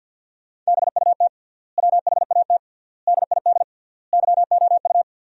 Morse code, heard as a keyed tone, about 50 wpm, 700 Hz. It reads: BUT WHAT DID YOU
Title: 3x Word-Spacing - Sentences from Top 100 Words - 50wpm